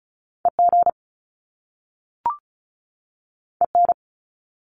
Morse code, heard as a keyed tone, about 36 wpm, 700 Hz. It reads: EG EN
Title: Top Two Letter Combinations in Words 35wpm